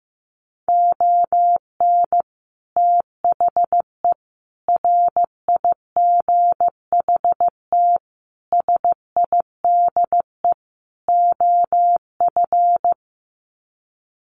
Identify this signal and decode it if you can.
Morse code, heard as a keyed tone, about 15 wpm, 700 Hz. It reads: ON THE RIGHT SIDE OF
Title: Sets of 5 Words Encore 15wpm